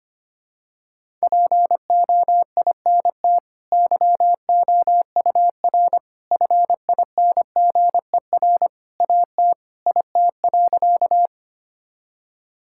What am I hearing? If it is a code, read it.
Morse code, heard as a keyed tone, about 25 wpm, 700 Hz. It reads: POINT YOUR FINGER AT IT.
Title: Sentences from Top 1600 Words 25wpm